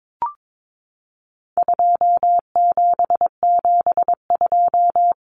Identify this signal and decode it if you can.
Morse code, heard as a keyed tone, about 22 wpm, 700 Hz. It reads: 2772
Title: Common Cut Numbers - 4-Digit 22wpm